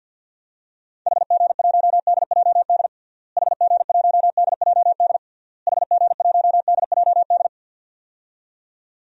Morse code, heard as a keyed tone, about 51 wpm, 700 Hz. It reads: HG1DJD HG1DJD HG1DJD
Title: DXCC-100 Call Signs - 6 Characters - Repeated 3x - 50wpm